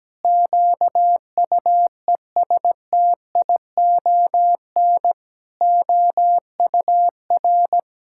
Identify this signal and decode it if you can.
Morse code, heard as a keyed tone, about 17 wpm, 700 Hz. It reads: QUESTION OUR
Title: Sentences from Top 1000 Words - Review Part 5 - 17wpm